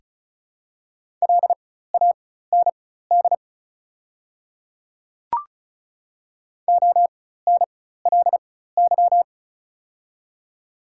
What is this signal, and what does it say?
Morse code, heard as a keyed tone, about 36 wpm, spaced as for 16 wpm, 700 Hz. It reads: LAND ONLY
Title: 4x Character Spacing - Top 300 Words - 35wpm